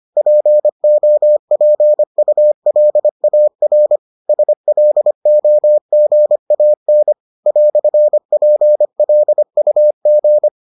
Morse code, heard as a keyed tone, about 25 wpm, 600 Hz. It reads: POPULAR SLOGAN "PLUG